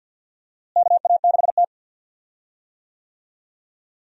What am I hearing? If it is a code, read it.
Morse code, heard as a keyed tone, about 52 wpm, 700 Hz. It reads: KA6T
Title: USA Call Signs - 4 Characters - Repeated 2x - 50wpm